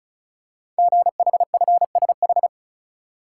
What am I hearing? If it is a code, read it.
Morse code, heard as a keyed tone, about 36 wpm, 690 Hz. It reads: GHFSH
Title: Code Groups - 5 Letters - 35wpm